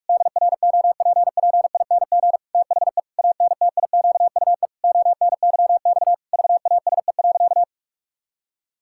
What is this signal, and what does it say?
Morse code, heard as a keyed tone, about 46 wpm, 700 Hz. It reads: DROPPING THE ANTIQUE ONYX VASE.
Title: Pangram Challenge v2 - 45wpm